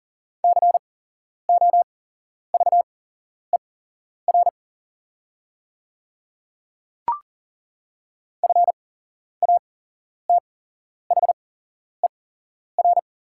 Morse code, heard as a keyed tone, about 40 wpm, spaced as for 11 wpm, 690 Hz. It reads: COVER FATHER
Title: 8x Character Spacing - Top 300 Words - 40wpm